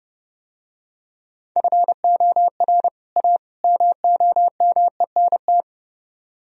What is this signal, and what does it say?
Morse code, heard as a keyed tone, about 31 wpm, 700 Hz. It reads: FOR A MOMENT